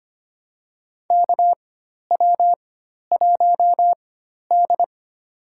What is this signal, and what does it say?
Morse code, heard as a keyed tone, about 25 wpm, spaced as for 12 wpm, 690 Hz. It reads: KW1D
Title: USA Call Signs - 4 Characters - 4x Character Spacing - 25wpm